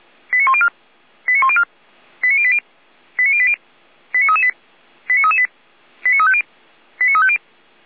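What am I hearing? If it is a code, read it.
Five-tone selective call (ZVEI), heard as quick runs of five tones, tones of about 70 ms. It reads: 89104 89104 89080 89080 89338 89338 89350 89350
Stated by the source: ZVEI five-tone call received by SDR, from feuerwehr-orlamuende.org fire brigade audio archive.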